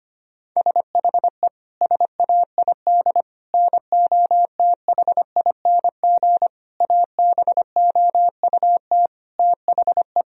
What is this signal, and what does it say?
Morse code, heard as a keyed tone, about 25 wpm, 700 Hz. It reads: SHE SAID NOTHING ABOUT THE
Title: Sentences from Top 1000 Words - Review Part 2 - 25wpm